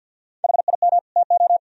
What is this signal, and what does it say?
Morse code, heard as a keyed tone, about 57 wpm, 700 Hz. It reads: HIM TO
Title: Speed-Racing - Sets of 2 Words 50wpm